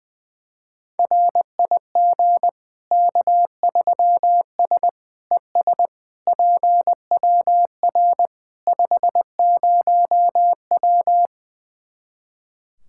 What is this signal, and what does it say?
Morse code, heard as a keyed tone, about 21 wpm, 700 Hz. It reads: RIG K3S ES PWR 50W
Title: Practice QSO - Exchange 2 - Info - 20wpm